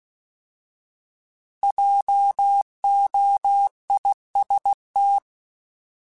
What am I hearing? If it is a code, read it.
Morse code, heard as a keyed tone, about 16 wpm, 800 Hz. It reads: JOIST
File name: Alphabetical / Alphabetical mp3s / Js @15